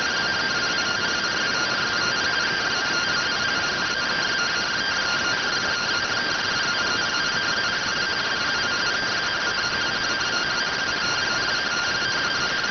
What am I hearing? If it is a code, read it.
Two-tone teleprinter signal, about 170 Hz shift.